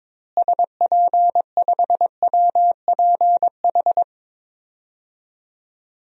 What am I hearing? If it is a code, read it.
Morse code, heard as a keyed tone, about 21 wpm, 700 Hz. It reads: SP5WPH
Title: DXCC-100 Call Signs - 6 Characters - Repeated 3x - 22wpm